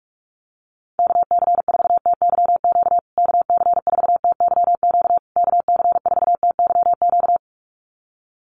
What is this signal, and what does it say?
Morse code, heard as a keyed tone, about 44 wpm, 690 Hz. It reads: KC4TYQ KC4TYQ KC4TYQ
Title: USA Call Signs Repeated 3x 45wpm